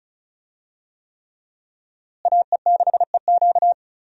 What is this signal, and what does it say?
Morse code, heard as a keyed tone, about 34 wpm, 700 Hz. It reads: AE6EQ
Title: USA Call Signs - 5 Characters - Repeated 3x - 35wpm